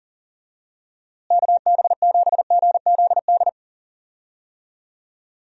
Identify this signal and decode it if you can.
Morse code, heard as a keyed tone, about 40 wpm, 700 Hz. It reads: KB7GZD
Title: USA Call Signs - 6 Characters - Repeated 2x - 40wpm